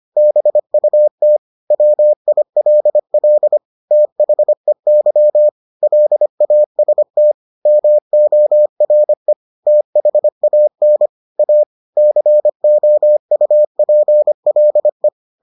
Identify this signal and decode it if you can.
Morse code, heard as a keyed tone, about 25 wpm, 600 Hz. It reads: BUT WILL THEY LAST MORE THAN A COUPLE